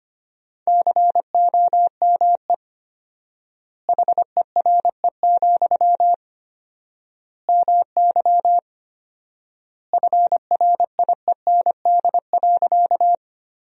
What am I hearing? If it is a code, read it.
Morse code, heard as a keyed tone, about 25 wpm, 700 Hz. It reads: COME HERE, MY FRIEND.
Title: 4x Word-Spacing - Sentences from Top 300 Words - 25wpm